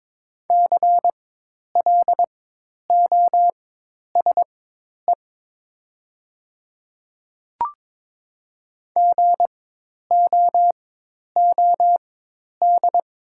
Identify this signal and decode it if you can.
Morse code, heard as a keyed tone, about 22 wpm, spaced as for 10 wpm, 700 Hz. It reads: CLOSE GOOD
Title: 4x Character Spacing - Top 300 Words - 22wpm